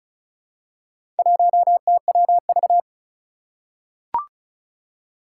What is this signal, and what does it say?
Morse code, heard as a keyed tone, about 36 wpm, 700 Hz. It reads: 1TWV
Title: Four Challenging Letter-Number Combinations 35wpm